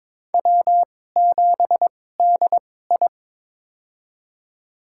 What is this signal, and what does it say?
Morse code, heard as a keyed tone, about 22 wpm, spaced as for 16 wpm, 700 Hz. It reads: W7DI
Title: USA Call Signs - 2x Character Spacing - 22wpm